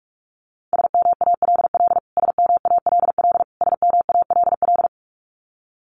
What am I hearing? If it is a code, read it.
Morse code, heard as a keyed tone, about 44 wpm, 700 Hz. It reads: SMALL SMALL SMALL
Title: Top 200 Words Repeated 45wpm